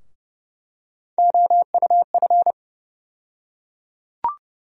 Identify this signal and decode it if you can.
Morse code, heard as a keyed tone, about 30 wpm, 700 Hz. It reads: OUF
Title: USA Call Sign Postfixes - 3 Letters 30wpm